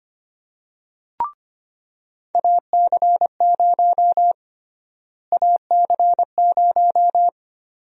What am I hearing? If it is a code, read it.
Morse code, heard as a keyed tone, about 25 wpm, 700 Hz. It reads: AC0 AC0
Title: USA Call Sign Prefixes - Repeated 2x - 25wpm